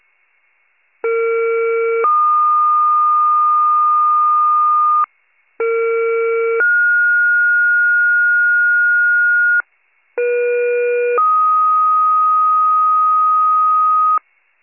Two-tone page, heard as a one-second tone, then a three-second tone, about 460 then 1220 Hz, about 460 then 1430 Hz, about 480 then 1250 Hz. It is heard from 1.0 to 14.2 s.